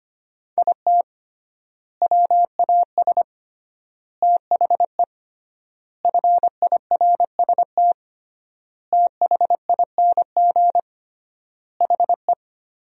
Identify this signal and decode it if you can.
Morse code, heard as a keyed tone, about 25 wpm, 700 Hz. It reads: IT WAS THE FIRST THING HE
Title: 3x Word-Spacing - Sentences from Top 300 Words - 25wpm